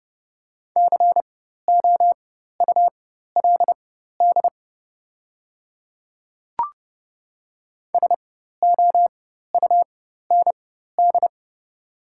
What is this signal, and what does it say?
Morse code, heard as a keyed tone, about 30 wpm, spaced as for 14 wpm, 700 Hz. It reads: COULD SOUND
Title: 4x Character Spacing - Top 100 Words - 30wpm